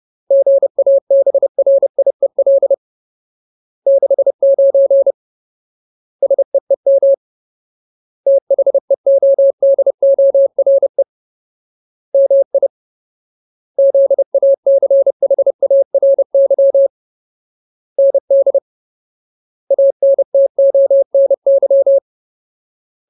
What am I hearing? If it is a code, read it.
Morse code, heard as a keyed tone, about 30 wpm, 550 Hz. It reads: GABRIEL 69 SEEM THEODORE MI ZACHARY ND ANTONY
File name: Ultimate Word List / 30WPM Ultimate